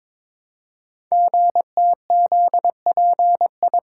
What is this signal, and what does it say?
Morse code, heard as a keyed tone, about 23 wpm, 700 Hz. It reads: GTZPI
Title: Code Groups - 5 Letters - 22wpm